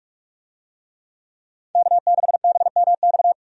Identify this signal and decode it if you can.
Morse code, heard as a keyed tone, about 46 wpm, 700 Hz. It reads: K6BMX
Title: USA Call Signs - 5 Characters - 45wpm